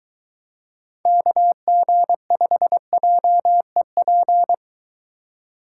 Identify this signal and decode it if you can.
Morse code, heard as a keyed tone, about 23 wpm, 700 Hz. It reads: KG5JEP